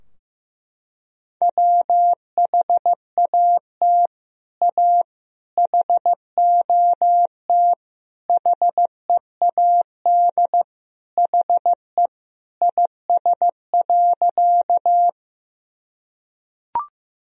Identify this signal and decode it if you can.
Morse code, heard as a keyed tone, about 15 wpm, 700 Hz. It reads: WHAT A HOT HEAD HE IS.
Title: Sentences from Top 1000 Words - Review Part 3 - 15wpm